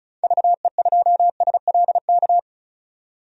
Morse code, heard as a keyed tone, about 36 wpm, 700 Hz. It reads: VE2SLK
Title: DXCC-100 Call Signs - 6 Characters - 35wpm